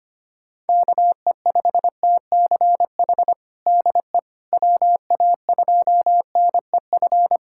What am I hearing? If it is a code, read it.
Morse code, heard as a keyed tone, about 25 wpm, 700 Hz. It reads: KE5TCH DE WA2NEF
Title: USA Callsign DE Callsign 25wpm